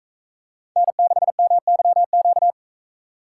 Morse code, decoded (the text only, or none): N6MYQ